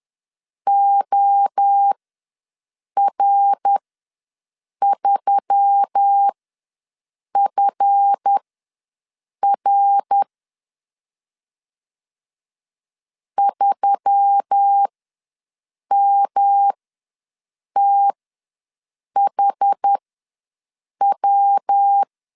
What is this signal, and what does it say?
Morse code, heard as a keyed tone, about 10 wpm, spaced as for 6 wpm, 800 Hz. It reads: OR3FR 3MTHW
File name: LEZIONI / 7 GF9Z / 7 GF9 ws 10wpm 4034